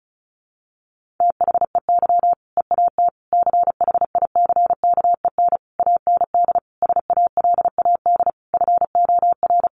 Morse code, heard as a keyed tone, about 34 wpm, 700 Hz. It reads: THEY EAT CHICKEN AND SALAD FOR
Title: Easy Phrases 35wpm